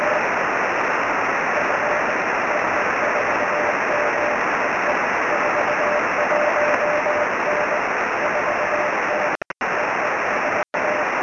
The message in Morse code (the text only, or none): none